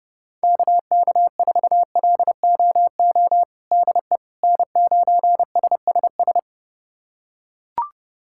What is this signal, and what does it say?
Morse code, heard as a keyed tone, about 30 wpm, 700 Hz. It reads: KK4LOO DE N9SSS
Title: USA Callsign DE Callsign 30wpm